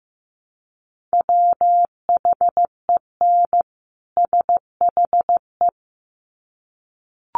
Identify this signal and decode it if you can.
Morse code, heard as a keyed tone, about 15 wpm, 700 Hz. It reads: WHEN SHE
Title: Sets of 2 Words 15wpm